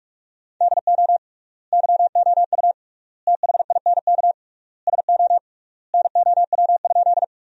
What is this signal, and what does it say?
Morse code, heard as a keyed tone, about 46 wpm, 700 Hz. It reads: DO YOU THINK SO NOW?
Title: 3x Word-Spacing - Sentences from Top 200 Words - 45wpm